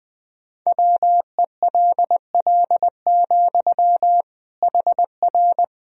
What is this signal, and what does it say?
Morse code, heard as a keyed tone, about 20 wpm, 700 Hz. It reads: WELL, HR
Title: Sentences from Top 200 Words and Abbreviations 20wpm